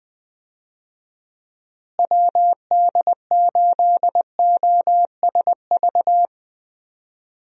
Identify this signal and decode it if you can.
Morse code, heard as a keyed tone, about 20 wpm, 700 Hz. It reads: WD8OSV
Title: USA Call Signs - 6 Characters - Repeated 3x - 20wpm